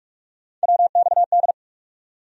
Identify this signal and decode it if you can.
Morse code, heard as a keyed tone, about 46 wpm, 700 Hz. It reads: WXD
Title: USA Call Sign Postfixes - 3 Letters 45wpm